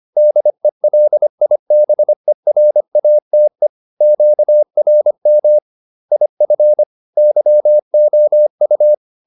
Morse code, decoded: DELIBERATE QRM IF YOU

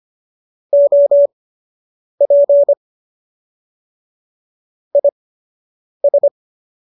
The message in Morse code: OP IS

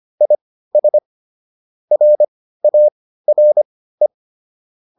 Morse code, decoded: IS RARE

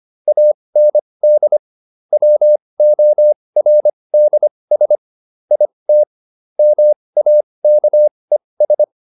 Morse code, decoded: AND WORDS IT MAKES

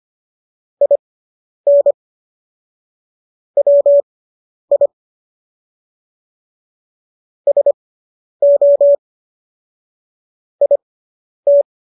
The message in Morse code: IN WI SO IT

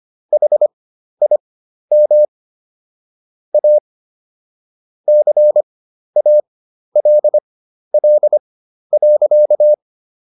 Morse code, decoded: HIM A CALL.

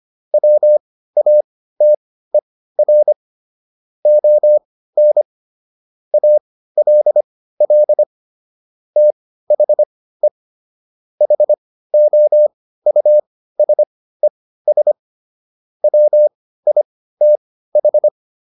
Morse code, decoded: WATER ON ALL THE HOUSES WITH

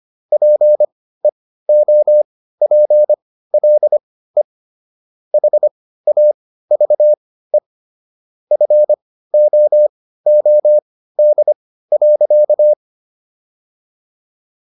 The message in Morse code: PEOPLE HAVE FOOD.

